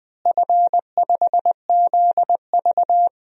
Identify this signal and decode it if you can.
Morse code, heard as a keyed tone, about 20 wpm, 700 Hz. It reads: F5ZV
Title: DXCC-100 Call Signs - 4 Characters - Repeated 2x - 20wpm